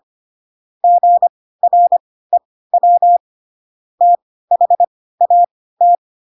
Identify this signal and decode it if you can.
Morse code, heard as a keyed tone, about 25 wpm, spaced as for 16 wpm, 700 Hz. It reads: GREW THAT
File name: Pinocchio / Pinocchio ch3 10 25 ik1hge